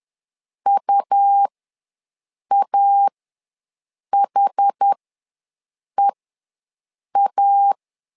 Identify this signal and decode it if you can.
Morse code, heard as a keyed tone, about 10 wpm, spaced as for 6 wpm, 800 Hz. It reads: UAHEA
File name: LEZIONI / 2 UHDN / 2 UHDN ws 10wpm 2044